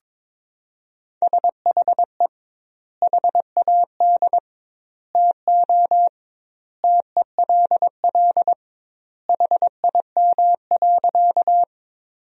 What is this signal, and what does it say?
Morse code, heard as a keyed tone, about 21 wpm, 700 Hz. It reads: SHE HAD TO TELL HIM.
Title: 2x Word-Spacing - Sentences from Top 200 Words - 22wpm